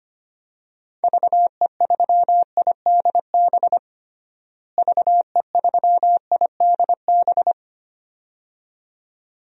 Morse code, heard as a keyed tone, about 25 wpm, 700 Hz. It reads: VE3IDB VE3IDB
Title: DXCC-100 Call Signs - 6 Characters - Repeated 2x - 25wpm